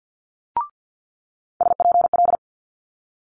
Morse code, heard as a keyed tone, about 50 wpm, 700 Hz. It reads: SPL
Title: Multi-letter Phonemes Quick Fire 50wpm